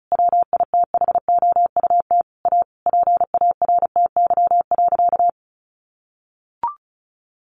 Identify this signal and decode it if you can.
Morse code, heard as a keyed tone, about 34 wpm, 700 Hz. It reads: WITHOUT A PARTY.
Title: Sentences from Top 1500 Words 35wpm